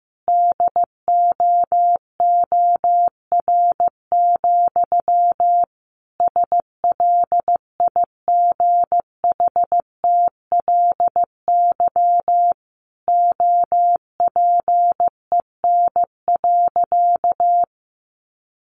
Morse code, heard as a keyed tone, about 15 wpm, 700 Hz. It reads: DOOR, SLIGHTLY OPEN.
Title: Sentences from Top 1300 Words 15wpm